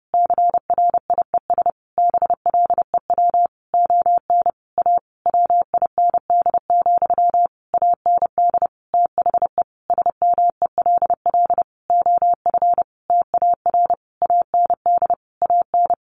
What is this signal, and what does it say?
Morse code, heard as a keyed tone, about 30 wpm, 700 Hz. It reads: CRIES BLEW ON A WIND, AND THE SMELL OF TAR AND AN
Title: A Sound Of Thunder 30wpm